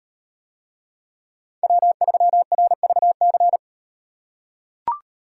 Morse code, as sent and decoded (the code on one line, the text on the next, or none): .-- ...-- .-. ...- -.-.
W3RVC